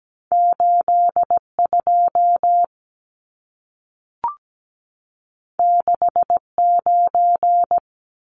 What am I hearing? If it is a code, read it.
Morse code, heard as a keyed tone, about 17 wpm, 700 Hz. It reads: 82 69
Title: Numbers - 2-Digit 17wpm